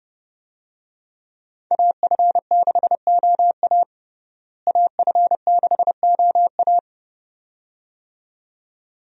Morse code, heard as a keyed tone, about 29 wpm, 700 Hz. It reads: AF6OA AF6OA
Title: USA Call Signs - 5 Characters - Repeated 2x - 30wpm